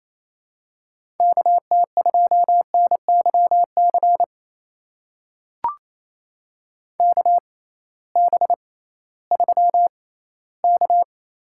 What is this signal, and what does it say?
Morse code, heard as a keyed tone, about 28 wpm, spaced as for 16 wpm, 700 Hz. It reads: KT2NYC KB3K
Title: USA Call Signs - 6 Characters - 6x Character Spacing - 28wpm